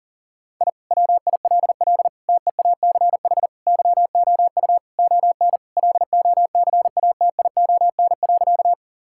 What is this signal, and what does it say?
Morse code, heard as a keyed tone, about 37 wpm, 700 Hz. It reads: I WILL TEACH YOU ON LOCATION.
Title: Sentences from Top 1000 Words - Review Part 6 - 40wpm